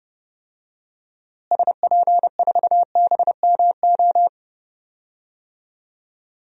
Morse code, heard as a keyed tone, about 30 wpm, 700 Hz. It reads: SP4BMO
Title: DXCC-100 Call Signs - 6 Characters - Repeated 3x - 30wpm